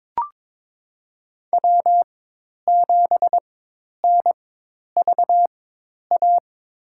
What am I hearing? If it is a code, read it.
Morse code, heard as a keyed tone, about 23 wpm, spaced as for 10 wpm, 700 Hz. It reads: W7NVA
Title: USA Call Signs - 5 Characters - 4x Character Spacing - 22wpm